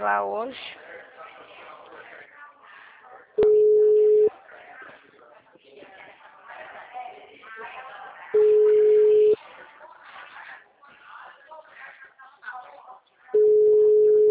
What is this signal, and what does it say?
Ringback tone (Europe, 425 Hz), with speech.
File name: out- -2050- - - .25324